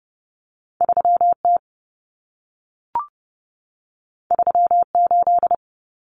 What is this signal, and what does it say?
Morse code, heard as a keyed tone, about 30 wpm, 700 Hz. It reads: 3T 38
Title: Common Cut Numbers - 2-Digit 30wpm